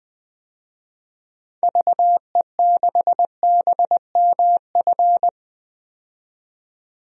Morse code, heard as a keyed tone, about 20 wpm, 700 Hz. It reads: VE6BMF
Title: DXCC-100 Call Signs - 6 Characters - 20wpm